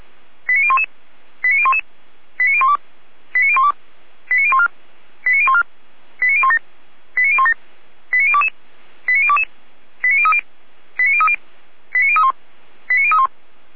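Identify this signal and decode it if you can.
Five-tone selective call (ZVEI), heard as quick runs of five tones, tones of about 80 ms. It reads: 89011 89011 89012 89012 89014 89014 89017 89017 89022 89022 89030 89030 89031 89031
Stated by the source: ZVEI five-tone selective call 89001 received with an SDR.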